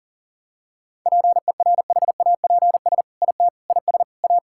Morse code, heard as a keyed tone, about 40 wpm, 700 Hz. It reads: PERHAPS IT IS A
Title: Sentences from Top 1000 Words - Review Part 5 - 40wpm